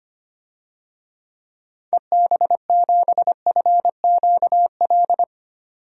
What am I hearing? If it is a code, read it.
Morse code, heard as a keyed tone, about 25 wpm, 700 Hz. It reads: EB7FQL
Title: DXCC-100 Call Signs - 6 Characters - 25wpm